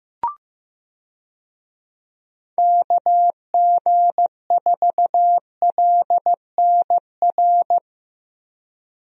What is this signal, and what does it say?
Morse code, heard as a keyed tone, about 16 wpm, 700 Hz. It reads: KG4LNR
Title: USA Call Signs - 6 Characters - Repeated 3x - 15wpm